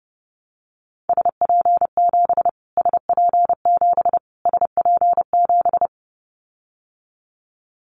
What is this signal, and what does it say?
Morse code, heard as a keyed tone, about 30 wpm, 700 Hz. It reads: SP7 SP7 SP7
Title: Callsign Prefixes for Top 25 DXCC - Repeated 3x - 30wpm